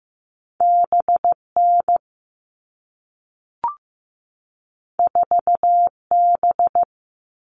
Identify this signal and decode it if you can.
Morse code, heard as a keyed tone, about 15 wpm, 700 Hz. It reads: BN 4B
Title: All Cut Numbers - 2-Digit 15wpm